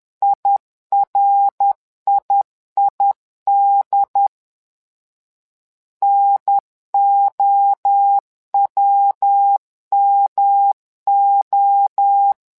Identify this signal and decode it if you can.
Morse code, heard as a keyed tone, about 10 wpm, 800 Hz. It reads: IRIID NOWMO
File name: LEZIONI / 3 MWRO / 3 MWRO 10wpm 2148